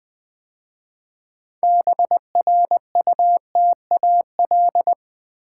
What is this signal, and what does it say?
Morse code, heard as a keyed tone, about 20 wpm, 700 Hz. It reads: BRUTAL
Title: 6 Letter Words - First 3 Letters Given - 20wpm